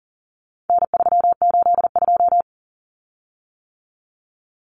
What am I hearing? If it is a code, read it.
Morse code, heard as a keyed tone, about 40 wpm, 700 Hz. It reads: N382
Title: Common Cut Numbers - 4-Digit 40wpm